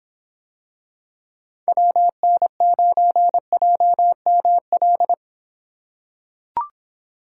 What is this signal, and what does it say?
Morse code, heard as a keyed tone, about 26 wpm, 700 Hz. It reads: WN9JML